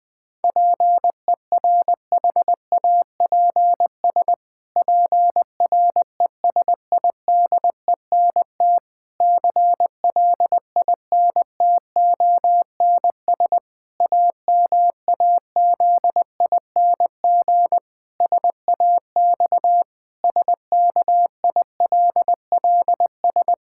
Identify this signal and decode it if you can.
Morse code, heard as a keyed tone, about 20 wpm, 700 Hz. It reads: PERHAPS PRESIDENT CLINTONS AMAZING SAX SKILLS
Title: Pangram Challenge v2 - 20wpm